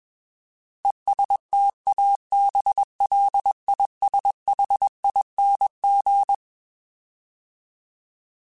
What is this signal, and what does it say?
Morse code, heard as a keyed tone, about 21 wpm, 800 Hz. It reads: ESTABLISHING
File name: Alphabetical / Alphabetical mp3s / Es @20